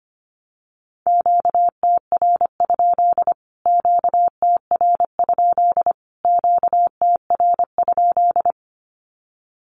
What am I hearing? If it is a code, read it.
Morse code, heard as a keyed tone, about 25 wpm, 700 Hz. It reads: QTR? QTR? QTR?
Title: Q-Codes Repeated 3x 25wpm